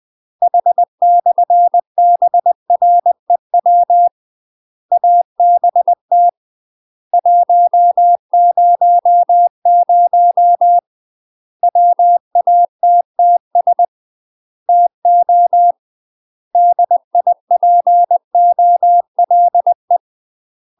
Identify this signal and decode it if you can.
Morse code, heard as a keyed tone, about 20 wpm, 700 Hz. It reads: H/BREW ABT 100 WATTS TO DIPOLE
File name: UK MASTER QSOs TRAINING / 20WPM / UK MASTER QSOs 0035